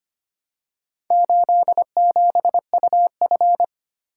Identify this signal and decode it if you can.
Morse code, heard as a keyed tone, about 25 wpm, 700 Hz. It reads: 87UF
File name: practice / 25wpm / Four Letter-Number Combinations - 25wpm